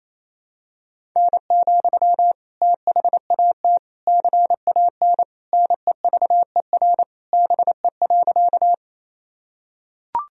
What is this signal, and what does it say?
Morse code, heard as a keyed tone, about 28 wpm, 700 Hz. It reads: N, THAT CAN NEVER BE.
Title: Sentences from Top 300 Words and Abbreviations 28wpm